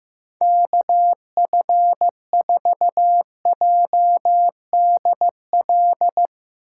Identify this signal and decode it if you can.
Morse code, heard as a keyed tone, about 16 wpm, 700 Hz. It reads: KF4JDL
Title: Speed Racing - USA 2x3 Call Signs 15wpm